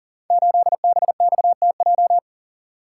Morse code, heard as a keyed tone, about 42 wpm, 700 Hz. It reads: 8BXTJ
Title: Code Groups - 5 Letter:Numbers - 40wpm